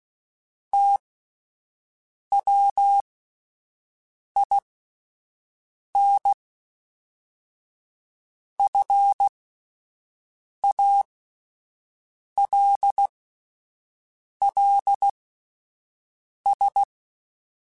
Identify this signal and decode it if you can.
Morse code, heard as a keyed tone, about 16 wpm, spaced as for 5 wpm, 800 Hz. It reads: TWIN FALLS